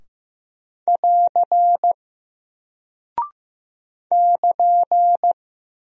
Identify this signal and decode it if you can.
Morse code, heard as a keyed tone, about 15 wpm, 700 Hz. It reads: <AR> <KN>
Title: Single Character (Letter:Number:Punctuation:Prosign) Alt 15wpm